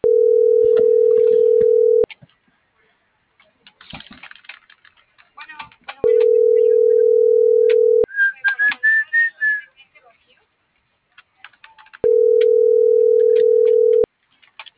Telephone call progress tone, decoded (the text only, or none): ringback tone